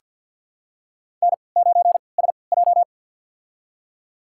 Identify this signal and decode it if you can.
Morse code, heard as a keyed tone, about 50 wpm, spaced as for 28 wpm, 700 Hz. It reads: N9SJ
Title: USA Call Signs - 4 Characters - 3x Character Spacing - 50wpm